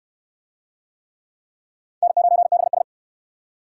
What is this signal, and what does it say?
Morse code, heard as a keyed tone, about 68 wpm, 700 Hz. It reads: N9BS